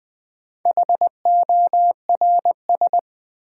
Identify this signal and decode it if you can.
Morse code, heard as a keyed tone, about 20 wpm, 700 Hz. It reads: HORS
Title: Build A Word - 1 Letter At A Time - 5-Letter Words - 20wpm